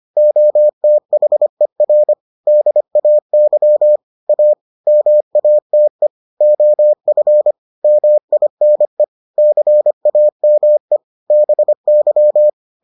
Morse code, heard as a keyed tone, about 25 wpm, 600 Hz. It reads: OTHER DAY A MATE OF MINE CAME BY